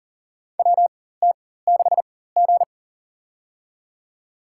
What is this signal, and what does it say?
Morse code, heard as a keyed tone, about 40 wpm, spaced as for 19 wpm, 700 Hz. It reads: WT6G
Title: USA Call Signs - 4 Characters - 4x Character Spacing - 40wpm